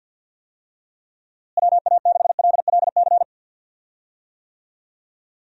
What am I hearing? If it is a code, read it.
Morse code, heard as a keyed tone, about 47 wpm, 700 Hz. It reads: WA6LLC
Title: USA Call Signs - 6 Characters - Repeated 3x - 50wpm